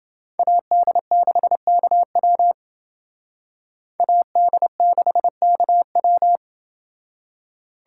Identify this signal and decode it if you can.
Morse code, heard as a keyed tone, about 29 wpm, 700 Hz. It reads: AD6KW AD6KW